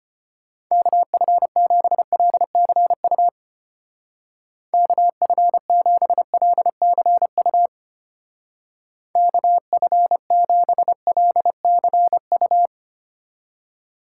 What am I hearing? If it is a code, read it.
Morse code, heard as a keyed tone, about 31 wpm, 700 Hz. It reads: KF7LCU KF7LCU KF7LCU
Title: Speed Racing - USA Call Signs 25wpm